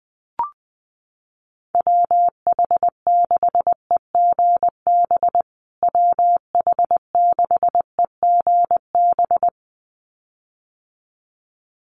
Morse code, heard as a keyed tone, about 20 wpm, 700 Hz. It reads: WH6EGB WH6EGB
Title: USA Call Signs Repeated 2x 20wpm